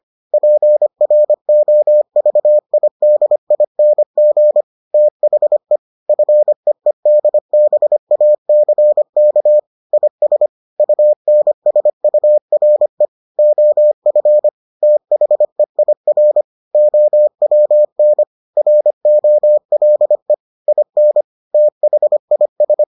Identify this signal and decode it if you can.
Morse code, heard as a keyed tone, about 25 wpm, 600 Hz. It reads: PROVIDING THE FEEDBACK IS UNSURE OF THEIR OWN ROLE IN THIS